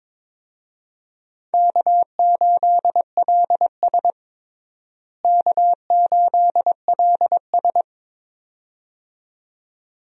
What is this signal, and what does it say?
Morse code, heard as a keyed tone, about 21 wpm, 700 Hz. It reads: K8LS K8LS
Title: USA Call Signs - 4 Characters - Repeated 2x - 22wpm